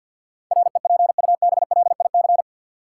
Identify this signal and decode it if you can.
Morse code, heard as a keyed tone, about 49 wpm, 700 Hz. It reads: REPUBLIC
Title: Sentences from Top 1500 Words - New Word Drill Alt - 50wpm